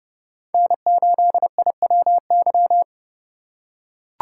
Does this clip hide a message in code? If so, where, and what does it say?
Yes: Morse code from 0.5 to 2.8 s, heard as a keyed tone, about 29 wpm, 700 Hz. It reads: N8IWY